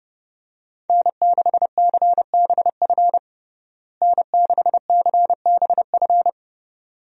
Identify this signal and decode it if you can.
Morse code, heard as a keyed tone, about 29 wpm, 700 Hz. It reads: N6CBF N6CBF
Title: USA Call Signs - 5 Characters - Repeated 2x - 30wpm